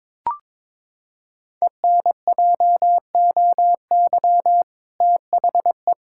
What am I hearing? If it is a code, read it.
Morse code, heard as a keyed tone, about 23 wpm, 700 Hz. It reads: ENJOY THE
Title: Sentences from Top 1000 Words - Review Part 5 - 22wpm